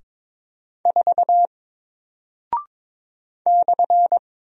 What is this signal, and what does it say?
Morse code, heard as a keyed tone, about 22 wpm, 700 Hz. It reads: 4 /
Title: Single Character (Letter:Number:Punctuation:Prosign) Alt 22wpm